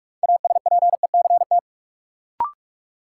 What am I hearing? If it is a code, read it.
Morse code, heard as a keyed tone, about 46 wpm, 700 Hz. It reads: ASPECT
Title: Sentences from Top 1300 Words - New Word Drill - 45wpm